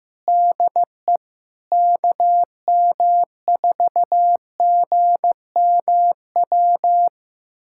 Morse code, heard as a keyed tone, about 15 wpm, 700 Hz. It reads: DE KM4GMW
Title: USA Callsign DE Callsign - 6 Characters - 15wpm